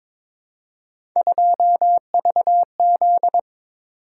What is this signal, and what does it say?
Morse code, heard as a keyed tone, about 22 wpm, 700 Hz. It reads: 2VZ
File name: practice / 22wpm / Three Letter-Number Combinations - 22wpm